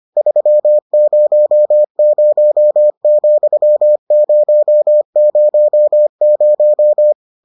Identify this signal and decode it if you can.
Morse code, heard as a keyed tone, about 25 wpm, 600 Hz. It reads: 300,000